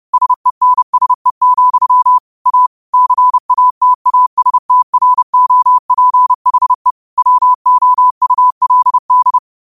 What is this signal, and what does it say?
Morse code, heard as a keyed tone, about 30 wpm, 1000 Hz. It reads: SENSE, A CATASTROPHE WOULD
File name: qotdfiles / 30wpm